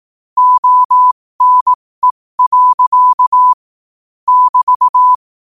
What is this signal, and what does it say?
Morse code, heard as a keyed tone, about 18 wpm, spaced as for 15 wpm, 1000 Hz. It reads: ONE. <BT>
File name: qotdfiles / 15wpm